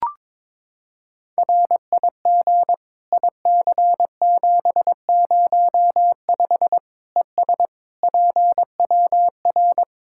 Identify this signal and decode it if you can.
Morse code, heard as a keyed tone, about 22 wpm, 700 Hz. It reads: RIG IC705 ES PWR
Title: Practice QSO - Exchange 2 - Info - 22wpm